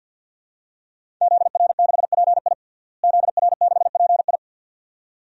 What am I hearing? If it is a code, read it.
Morse code, heard as a keyed tone, about 51 wpm, 700 Hz. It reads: ZR6PI ZR6PI
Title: DXCC-100 Call Signs - 5 Characters - Repeated 2x - 50wpm